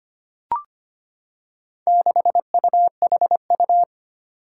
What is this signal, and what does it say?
Morse code, heard as a keyed tone, about 25 wpm, 700 Hz. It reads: 6UHU
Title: Four Challenging Letter-Number Combinations 25wpm